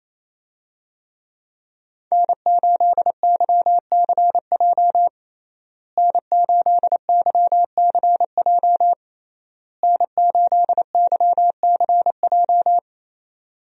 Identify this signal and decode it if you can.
Morse code, heard as a keyed tone, about 28 wpm, 700 Hz. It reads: N8YCJ N8YCJ N8YCJ
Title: USA Call Signs - 5 Characters - Repeated 3x - 28wpm